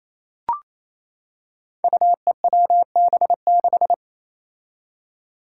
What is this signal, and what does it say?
Morse code, heard as a keyed tone, about 27 wpm, 700 Hz. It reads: UEWB6
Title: Five Challenging Letter-Number Combinations 28wpm